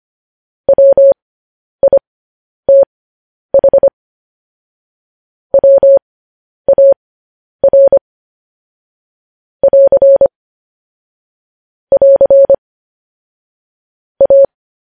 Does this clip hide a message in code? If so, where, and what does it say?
Yes: Morse code from 0.7 to 14.4 s, heard as a keyed tone, about 25 wpm, spaced as for 10 wpm, 550 Hz. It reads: WITH WAR <AR> <AR> A